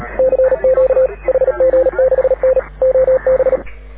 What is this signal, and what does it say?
Morse code, heard as a keyed tone, about 37 wpm, 530 Hz. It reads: CQ HG6N OB
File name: AUDIO2019 / HG6N CW SSB MIX